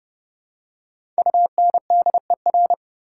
Morse code, heard as a keyed tone, about 31 wpm, 700 Hz. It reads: UNDER
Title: Prefixes and Suffixes 30wpm